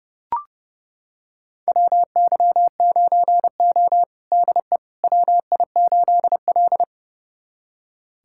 Morse code, heard as a keyed tone, about 32 wpm, 700 Hz. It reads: WY9O DE WI8L
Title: USA Callsign DE Callsign - 4 Characters - 30wpm